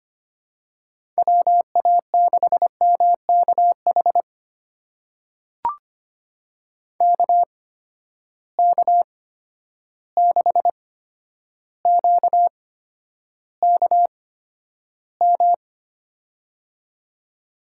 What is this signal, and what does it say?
Morse code, heard as a keyed tone, about 25 wpm, spaced as for 11 wpm, 700 Hz. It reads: WA6MKH KK6QKM
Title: USA Call Signs - 6 Characters - 8x Character Spacing - 25wpm